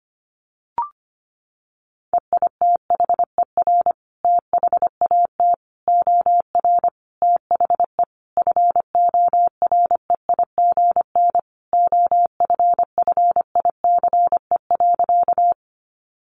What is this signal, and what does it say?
Morse code, heard as a keyed tone, about 25 wpm, 700 Hz. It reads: EITHER THAT OR THE FOREIGN OFFICE.
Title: Sentences from Top 1400 Words 25wpm